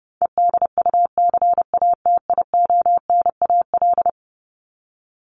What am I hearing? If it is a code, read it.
Morse code, heard as a keyed tone, about 29 wpm, 700 Hz. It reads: EDUCATIONAL
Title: Top 1001-1250 Words 30wpm